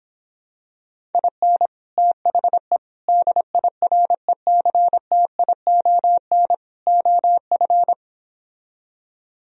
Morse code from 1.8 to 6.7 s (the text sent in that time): THE DIRECTION